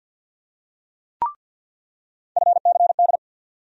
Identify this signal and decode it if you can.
Morse code, heard as a keyed tone, about 47 wpm, 700 Hz. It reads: FCD
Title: USA Call Sign Postfixes - 3 Letters 50wpm